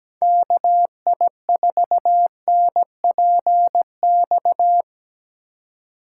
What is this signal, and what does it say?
Morse code, heard as a keyed tone, about 19 wpm, 700 Hz. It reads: KI4NPX
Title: Speed Racing - USA 2x3 Call Signs 17wpm